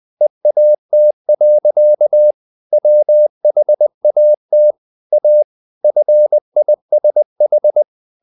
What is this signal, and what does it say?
Morse code, heard as a keyed tone, about 20 wpm, 600 Hz. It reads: EAT. WHAT A FISH
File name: the old man and the sea part 2 20wpm